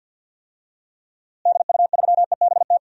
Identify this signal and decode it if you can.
Morse code, heard as a keyed tone, about 53 wpm, 700 Hz. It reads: DU3EBT
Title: DXCC-100 Call Signs - 6 Characters - Repeated 2x - 50wpm